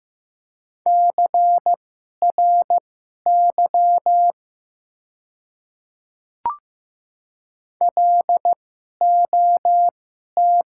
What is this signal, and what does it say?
Morse code, heard as a keyed tone, about 15 wpm, spaced as for 11 wpm, 690 Hz. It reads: CRY LOT
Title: 2x Character Spacing - Top 500 Words - 15wpm